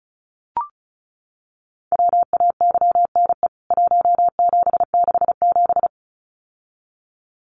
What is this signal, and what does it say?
Morse code, heard as a keyed tone, about 34 wpm, 700 Hz. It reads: WAYNE 1767
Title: CWT Exchange 35wpm